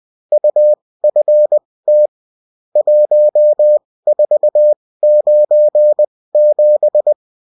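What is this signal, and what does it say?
Morse code, heard as a keyed tone, about 20 wpm, spaced as for 16 wpm, 600 Hz. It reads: UFT 1497
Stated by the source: Bulletin in text and CW form.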